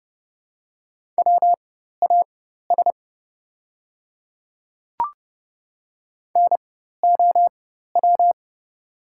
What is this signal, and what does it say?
Morse code, heard as a keyed tone, about 30 wpm, spaced as for 14 wpm, 700 Hz. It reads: WAS NOW
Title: 4x Character Spacing - Top 100 Words - 30wpm